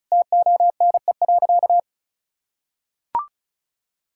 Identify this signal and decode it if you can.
Morse code, heard as a keyed tone, about 36 wpm, 700 Hz. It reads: TONE.
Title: Sentences from Top 1600 Words 35wpm